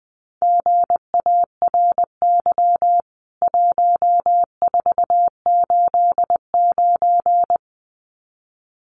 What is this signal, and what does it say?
Morse code, heard as a keyed tone, about 20 wpm, 700 Hz. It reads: GARY 1489
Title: CWT Exchange 20wpm